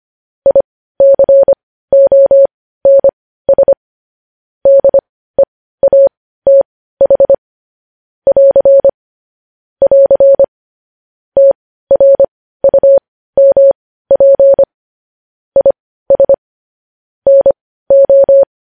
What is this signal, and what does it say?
Morse code, heard as a keyed tone, about 25 wpm, spaced as for 15 wpm, 550 Hz. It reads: ICONS DEATH <AR> <AR> TRUMP IS NO
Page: News Headlines in Morse Code at 15 WPM